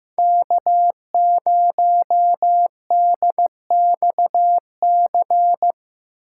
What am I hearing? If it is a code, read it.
Morse code, heard as a keyed tone, about 16 wpm, 700 Hz. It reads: K0DXC
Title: K1USN SST QSO Practice - 15wpm